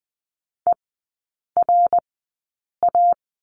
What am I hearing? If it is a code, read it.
Morse code, heard as a keyed tone, about 20 wpm, spaced as for 8 wpm, 700 Hz. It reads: ERA
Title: 087 Lesson 15e - Number 5 - Three Characters 20wpm